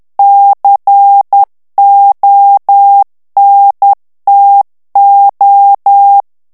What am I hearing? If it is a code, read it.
Morse code, heard as a keyed tone, about 10 wpm, 800 Hz. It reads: CONTO
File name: morse mp3 / ITALIEN texte clair 1 vitesse 10 mots minute